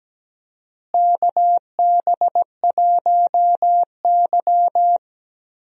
Morse code, heard as a keyed tone, about 17 wpm, 700 Hz. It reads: KB1Y
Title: USA Call Signs - 4 Characters - Repeated 3x - 17wpm